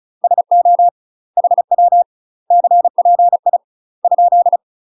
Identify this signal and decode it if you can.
Morse code, heard as a keyed tone, about 36 wpm, 700 Hz. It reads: SO HW CPI ?
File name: UK MASTER QSOs TRAINING/35WPM